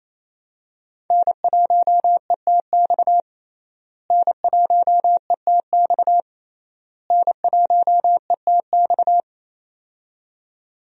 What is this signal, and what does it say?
Morse code, heard as a keyed tone, about 28 wpm, 700 Hz. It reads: N1ETX N1ETX N1ETX
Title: USA Call Signs - 5 Characters - Repeated 3x - 28wpm